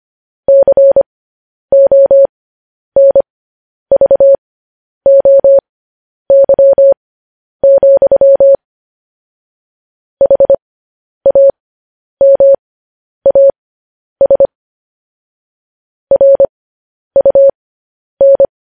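Morse code, heard as a keyed tone, about 25 wpm, spaced as for 10 wpm, 550 Hz. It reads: CONVOY, HAMAS RUN
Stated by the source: Morse code transmission of news headlines.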